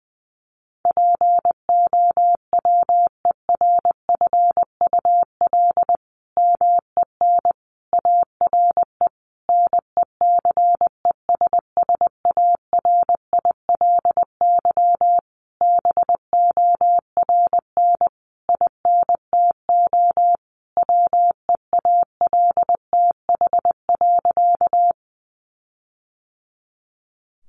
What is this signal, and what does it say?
Morse code, heard as a keyed tone, about 20 wpm, 700 Hz. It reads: POWERFUL MEN ARE NECESSARILY BORN INTO WEALTH.
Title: Sentences from Top 1400 Words 20wpm